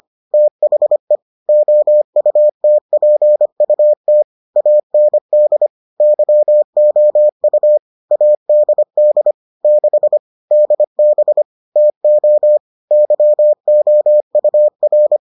Morse code, heard as a keyed tone, about 25 wpm, 600 Hz. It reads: THE OUTPUT AND YOU ADD 6 DB TO YOUR